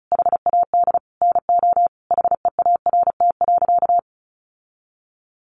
Morse code, heard as a keyed tone, about 34 wpm, 700 Hz. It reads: HAD NO HEART.
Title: Sentences from Top 1200 Words 35wpm